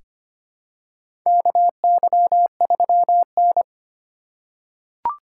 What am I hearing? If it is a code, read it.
Morse code, heard as a keyed tone, about 25 wpm, 700 Hz. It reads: KY3N
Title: USA Call Signs - 4 Characters - 3x Character Spacing - 25wpm